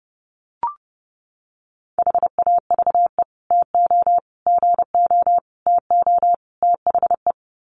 Morse code, heard as a keyed tone, about 30 wpm, 700 Hz. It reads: HAVE TO GO TO THE
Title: Sets of 5 Words Encore 30wpm